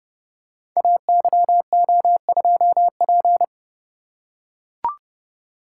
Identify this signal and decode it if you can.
Morse code, heard as a keyed tone, about 30 wpm, 700 Hz. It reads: AYO2P
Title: Code Groups - 5 Letter:Numbers - 30wpm